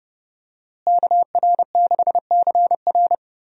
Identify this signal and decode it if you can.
Morse code, heard as a keyed tone, about 29 wpm, 700 Hz. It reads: KR6CR
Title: USA Call Signs - 5 Characters - Repeated 3x - 30wpm